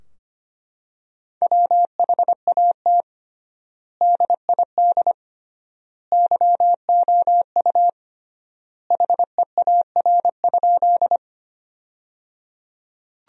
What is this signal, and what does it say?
Morse code, heard as a keyed tone, about 25 wpm, 700 Hz. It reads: WHAT DID YOU HEAR?
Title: 3x Word-Spacing - Sentences from Top 300 Words - 25wpm